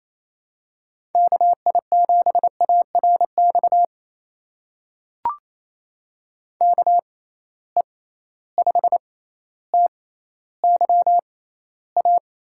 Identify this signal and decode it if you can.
Morse code, heard as a keyed tone, about 28 wpm, spaced as for 14 wpm, 700 Hz. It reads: KI7ARX KE5TYA
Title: USA Call Signs - 6 Characters - 6x Character Spacing - 28wpm